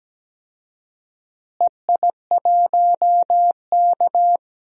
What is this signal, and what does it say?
Morse code, heard as a keyed tone, about 17 wpm, 700 Hz. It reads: EI1K
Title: DXCC-100 Call Signs - 4 Characters - Repeated 3x - 17wpm